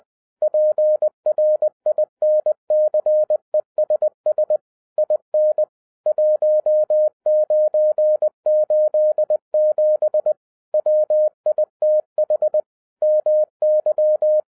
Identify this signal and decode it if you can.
Morse code, heard as a keyed tone, about 20 wpm, 610 Hz. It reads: PRINCESS IN 1987 WITH MY